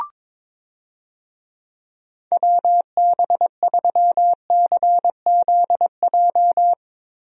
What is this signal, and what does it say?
Morse code, heard as a keyed tone, about 22 wpm, 700 Hz. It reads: WB3CZJ
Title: USA Call Signs - 6 Characters - Repeated 2x - 22wpm